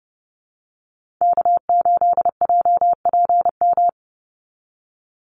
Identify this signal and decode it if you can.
Morse code, heard as a keyed tone, about 30 wpm, 700 Hz. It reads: K8JPM
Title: Call Signs Repeated 2x - 30wpm